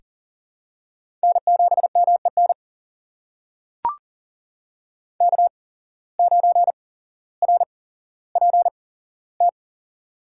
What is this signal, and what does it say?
Morse code, heard as a keyed tone, about 40 wpm, spaced as for 17 wpm, 700 Hz. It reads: N7MEN K9RPT
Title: USA Call Signs - 5 Characters - 8x Character Spacing - 40wpm